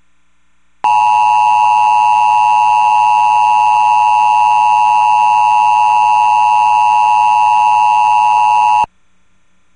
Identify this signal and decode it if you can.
Emergency Alert System (SAME) tones (the 853 and 960 Hz attention tone). It is heard from 0.8 to 8.9 s.